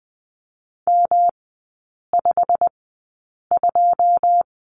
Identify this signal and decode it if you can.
Morse code, heard as a keyed tone, about 20 wpm, spaced as for 8 wpm, 700 Hz. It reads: M52
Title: 205 Lesson 33e - Letter j - Three Characters 20wpm